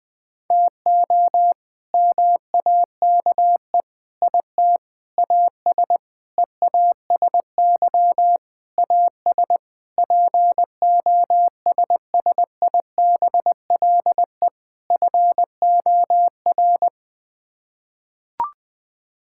Morse code, heard as a keyed tone, about 20 wpm, 700 Hz. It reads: TO MAKE IT AS EASY AS POSSIBLE FOR
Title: Sets of 8 Words - 20wpm